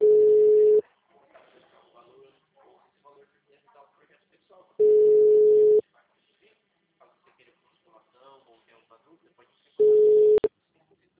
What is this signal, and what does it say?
Ringback tone (Europe, 425 Hz), with speech.